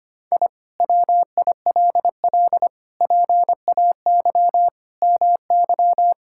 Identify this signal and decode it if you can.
Morse code, heard as a keyed tone, about 25 wpm, 700 Hz. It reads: I WILL PAY MY
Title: Sentences from Top 1000 Words - Review Part 5 - 25wpm